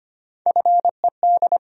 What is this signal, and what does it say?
Morse code, heard as a keyed tone, about 25 wpm, 700 Hz. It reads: FED
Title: USA Call Sign Postfixes - 3 Letters 25wpm